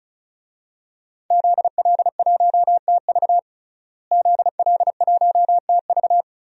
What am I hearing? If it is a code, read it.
Morse code, heard as a keyed tone, about 36 wpm, 700 Hz. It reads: ZL1TV ZL1TV
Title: DXCC-100 Call Signs - 5 Characters - Repeated 3x - 35wpm